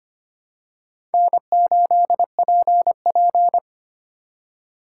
Morse code, heard as a keyed tone, about 25 wpm, 700 Hz. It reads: N8PP
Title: USA Call Signs - 4 Characters - 8x Character Spacing - 25wpm